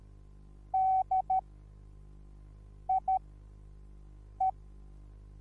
Morse code, heard as a keyed tone, about 13 wpm, spaced as for 5 wpm, 760 Hz. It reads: DIE
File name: Onlinelehrgang / morsen / DJ4UF Morsen21